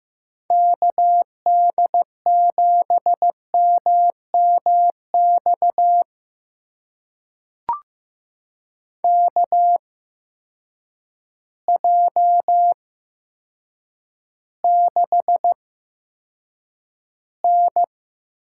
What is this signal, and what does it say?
Morse code, heard as a keyed tone, about 15 wpm, spaced as for 8 wpm, 700 Hz. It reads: KD7MMX KJ6N
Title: USA Call Signs - 6 Characters - 8x Character Spacing - 15wpm